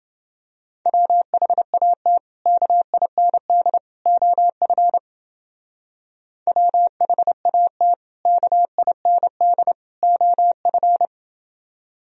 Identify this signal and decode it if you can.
Morse code, heard as a keyed tone, about 29 wpm, 700 Hz. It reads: WHAT KIND OF WHAT KIND OF